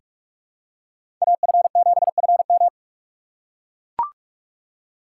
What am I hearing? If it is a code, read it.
Morse code, heard as a keyed tone, about 44 wpm, 700 Hz. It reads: AF7FM
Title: USA Call Signs - 5 Characters - 6x Character Spacing - 45wpm